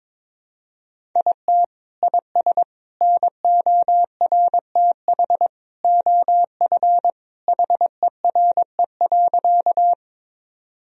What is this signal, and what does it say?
Morse code, heard as a keyed tone, about 22 wpm, 700 Hz. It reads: IT IS NORTH OF HERE.
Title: Sentences from Top 1000 Words - Review Part 1 - 22wpm